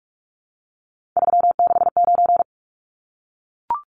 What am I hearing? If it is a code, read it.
Morse code, heard as a keyed tone, about 44 wpm, 700 Hz. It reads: 369
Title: Numbers - 3-Digit 45wpm